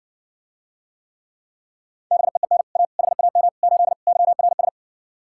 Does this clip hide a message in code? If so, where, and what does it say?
Yes: Morse code from 2.1 to 4.7 s, heard as a keyed tone, about 60 wpm, 700 Hz. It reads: BEEN A HAM 8 YRS